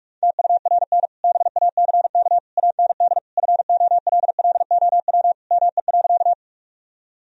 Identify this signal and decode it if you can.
Morse code, heard as a keyed tone, about 46 wpm, 700 Hz. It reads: TURN BACK AND FOLLOW ME.
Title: Sentences from Top 1000 Words - Review Part 1 - 45wpm